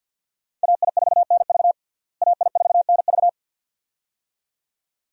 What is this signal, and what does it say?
Morse code, heard as a keyed tone, about 46 wpm, 700 Hz. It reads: AI4NV AI4NV
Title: USA Call Signs - 5 Characters - Repeated 2x - 50wpm